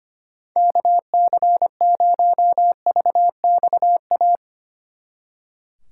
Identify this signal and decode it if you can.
Morse code, heard as a keyed tone, about 31 wpm, 700 Hz. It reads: KC0VXA
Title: Speed Racing - USA 2x3 Call Signs 25wpm